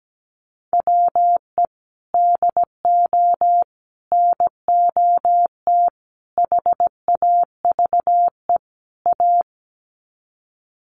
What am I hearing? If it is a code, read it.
Morse code, heard as a keyed tone, about 17 wpm, 700 Hz. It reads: WE DO NOT HAVE A
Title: Sets of 5 Words 17wpm